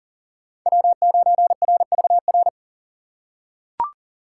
Morse code, heard as a keyed tone, about 40 wpm, 700 Hz. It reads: W9RVR
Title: USA Call Signs - 2x Character Spacing - 40wpm